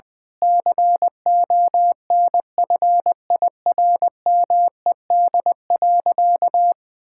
Morse code, heard as a keyed tone, about 21 wpm, 700 Hz. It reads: CONFIRMED.
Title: Sentences with 8-Word Phrases - 20wpm